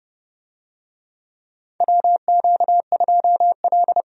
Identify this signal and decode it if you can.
Morse code, heard as a keyed tone, about 30 wpm, 700 Hz. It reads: WQ2L
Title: USA Call Signs - 4 Characters - Repeated 2x - 30wpm